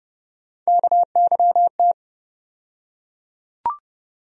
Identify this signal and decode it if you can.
Morse code, heard as a keyed tone, about 30 wpm, 700 Hz. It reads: KYT
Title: USA Call Sign Postfixes - 3 Letters 30wpm